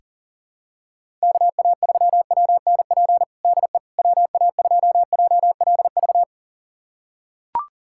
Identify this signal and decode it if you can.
Morse code, heard as a keyed tone, about 37 wpm, 700 Hz. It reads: KA3WNP DE WA2JLV
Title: USA Callsign DE Callsign - 6 Characters - 40wpm